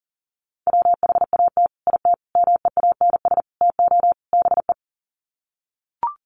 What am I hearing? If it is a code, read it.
Morse code, heard as a keyed tone, about 40 wpm, 700 Hz. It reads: WHAT IT MEANS TO BE
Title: Sets of 5 Words 40wpm